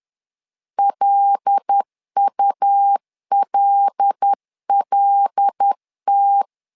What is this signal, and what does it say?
Morse code, heard as a keyed tone, about 10 wpm, 800 Hz. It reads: LULLT
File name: LEZIONI / 5 LBVC / 5 L 10wpm 2259